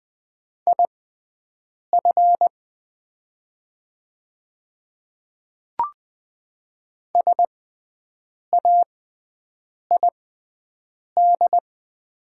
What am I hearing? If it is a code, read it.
Morse code, heard as a keyed tone, about 20 wpm, spaced as for 7 wpm, 700 Hz. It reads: IF SAID
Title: 6x Character Spacing - Top 100 Words - 20wpm